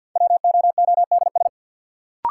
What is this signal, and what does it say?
Morse code, heard as a keyed tone, about 48 wpm, 700 Hz. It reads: WOODS
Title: Sentences from Top 1000 Words - Review Part 2 - New Word Drill - 50wpm